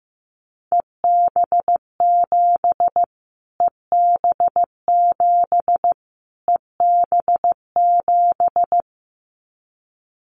Morse code, heard as a keyed tone, about 15 wpm, 700 Hz. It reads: EB7 EB7 EB7
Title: Callsign Prefixes for Top 25 DXCC - Repeated 3x - 15wpm